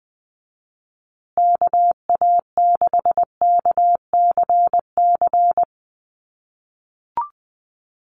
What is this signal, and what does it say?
Morse code, heard as a keyed tone, about 19 wpm, 700 Hz. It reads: KA6KCC
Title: USA Call Signs Repeated 2x 20wpm